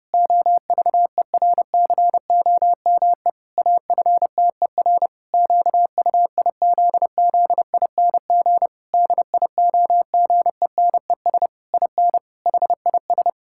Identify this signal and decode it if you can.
Morse code, heard as a keyed tone, about 30 wpm, 700 Hz. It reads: OVERCOME AFTER QUIZZING DIOGENES IN HIS
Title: Pangram Challenge v2 - 30wpm